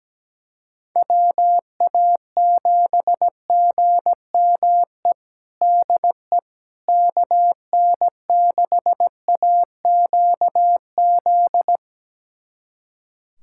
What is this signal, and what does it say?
Morse code, heard as a keyed tone, about 17 wpm, 700 Hz. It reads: WA7GME DE KN6AQZ
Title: USA Callsign DE Callsign 17wpm